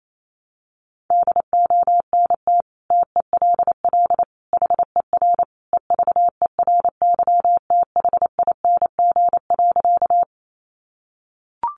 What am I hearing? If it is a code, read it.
Morse code, heard as a keyed tone, about 28 wpm, 700 Hz. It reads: DONT TELL HER EVERYTHING.
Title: Sentences from Top 700 Words 28wpm